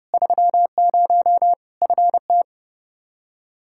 Morse code, heard as a keyed tone, about 30 wpm, 700 Hz. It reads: 30 FT
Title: Practice QSO - Exchange 2 - Info - 30wpm